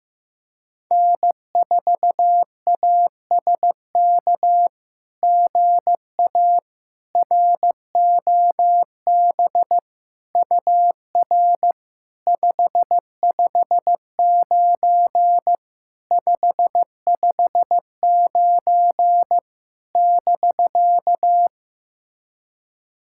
Morse code, heard as a keyed tone, about 15 wpm, 700 Hz. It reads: N4ASK GA ROB UR 559 559 <BK>
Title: SOTA QSO Practice 15wpm